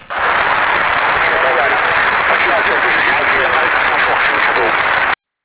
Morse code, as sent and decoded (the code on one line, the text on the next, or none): none